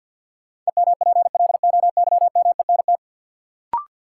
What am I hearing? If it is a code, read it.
Morse code, heard as a keyed tone, about 47 wpm, 700 Hz. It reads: EMPLOYMENT
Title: Sentences from Top 1500 Words - New Word Drill - 50wpm